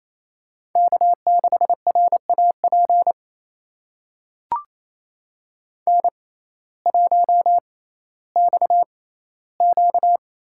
Morse code, heard as a keyed tone, about 28 wpm, spaced as for 15 wpm, 700 Hz. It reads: K6RAP N1XQ
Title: USA Call Signs - 6x Character Spacing - 28wpm